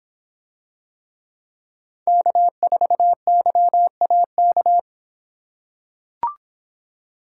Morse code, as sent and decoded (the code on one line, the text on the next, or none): -.- ....- -.-- .- -.-
K4YAK